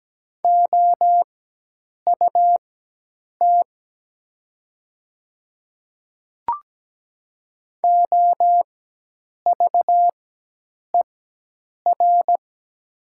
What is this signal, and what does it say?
Morse code, heard as a keyed tone, about 17 wpm, spaced as for 8 wpm, 700 Hz. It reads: OUT OVER
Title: 4x Character Spacing - Top 200 Words - 17wpm